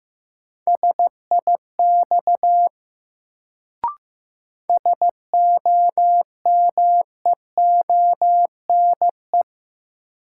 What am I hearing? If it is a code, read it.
Morse code, heard as a keyed tone, about 15 wpm, 700 Hz. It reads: SIX SOMEONE
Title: Sentences from Top 600 Words - New Word Drill - 15wpm